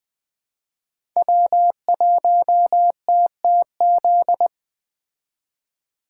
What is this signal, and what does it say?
Morse code, heard as a keyed tone, about 21 wpm, 700 Hz. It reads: W1TTZ